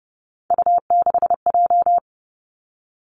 Morse code, heard as a keyed tone, about 30 wpm, 700 Hz. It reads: U6J
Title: Three Challenging Letter-Number Combinations 30wpm